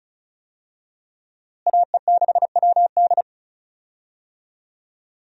Morse code, heard as a keyed tone, about 36 wpm, 700 Hz. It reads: AE6WD
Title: USA Call Signs - 5 Characters - Repeated 3x - 35wpm